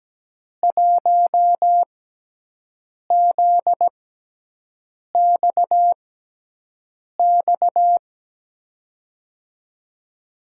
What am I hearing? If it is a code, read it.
Morse code, heard as a keyed tone, about 17 wpm, spaced as for 6 wpm, 700 Hz. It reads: 1ZXX
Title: USA Call Signs - 6x Character Spacing - 17wpm